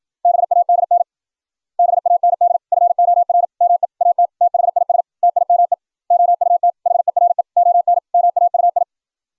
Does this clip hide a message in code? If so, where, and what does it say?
Yes: Morse code from 0.2 to 8.8 s, heard as a keyed tone, about 56 wpm, 700 Hz. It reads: BADN BAND FOR ME AT THIS TIME OUT HERE ON MAUI